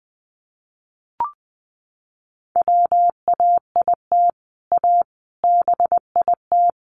Morse code, heard as a keyed tone, about 20 wpm, 700 Hz. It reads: WAIT A BIT
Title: Sentences from Top 900 Words 20wpm